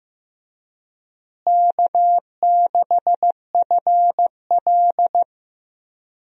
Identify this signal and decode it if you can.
Morse code, heard as a keyed tone, about 15 wpm, 700 Hz. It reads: K6FL
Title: USA Call Signs - 4 Characters - Repeated 3x - 15wpm